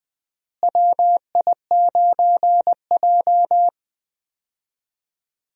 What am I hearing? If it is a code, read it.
Morse code, heard as a keyed tone, about 20 wpm, 700 Hz. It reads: WI9J
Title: USA Call Signs - 4 Characters - Repeated 3x - 20wpm